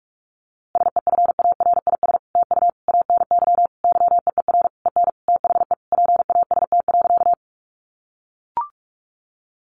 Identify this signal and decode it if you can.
Morse code, heard as a keyed tone, about 44 wpm, 700 Hz. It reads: SEFARIS TU ANY YEER EN THE PAST.
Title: A Sound Of Thunder 45wpm